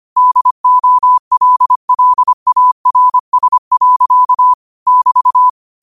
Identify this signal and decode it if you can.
Morse code, heard as a keyed tone, about 25 wpm, 1000 Hz. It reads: DOLLARS. <BT>